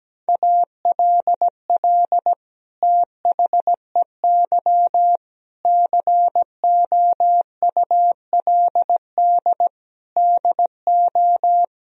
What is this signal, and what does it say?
Morse code, heard as a keyed tone, about 17 wpm, 700 Hz. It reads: ALL THEY COULD DO
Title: Sentences from Top 1000 Words - Review Part 1 - 17wpm